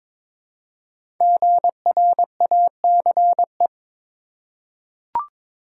Morse code, heard as a keyed tone, about 22 wpm, 700 Hz. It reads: GRACE
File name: 5 Letter Words - First 3 Letters Given - 22wpm